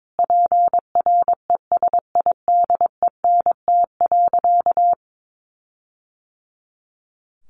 Morse code, heard as a keyed tone, about 22 wpm, 700 Hz. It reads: PRESIDENT.
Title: Sentences from Top 1500 Words 22wpm